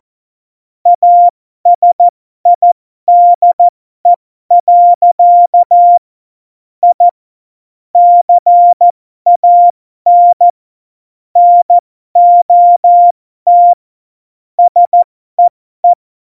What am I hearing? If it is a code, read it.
Morse code, heard as a keyed tone, about 14 wpm, spaced as for 12 wpm, 700 Hz. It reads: ASIDE. I CAN NOT SEE
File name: Morsekurs / Witze